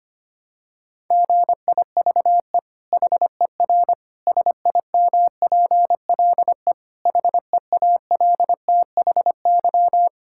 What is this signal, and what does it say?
Morse code, heard as a keyed tone, about 25 wpm, 700 Hz. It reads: GIVE HER SIMPLE HEALTHY
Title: Sentences from Top 1000 Words - Review Part 5 - 25wpm